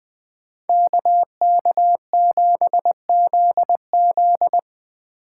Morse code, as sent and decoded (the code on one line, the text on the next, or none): -.- -.- --... --.. --..
KK7ZZ